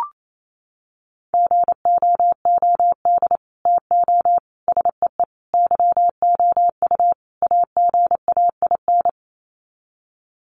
Morse code, heard as a keyed tone, about 28 wpm, 700 Hz. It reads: GOOD TO SEE YOU AGAIN
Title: Sets of 5 Words Encore2 28wpm